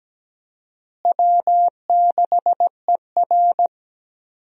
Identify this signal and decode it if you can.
Morse code, heard as a keyed tone, about 17 wpm, 700 Hz. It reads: W6ER
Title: USA Call Signs - 4 Characters - Repeated 2x - 17wpm